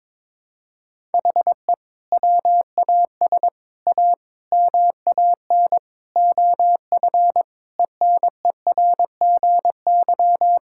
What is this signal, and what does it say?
Morse code, heard as a keyed tone, about 22 wpm, 700 Hz. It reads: HE WAS A MAN OF ENERGY
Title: Sentences from Top 1000 Words - Review Part 6 - 22wpm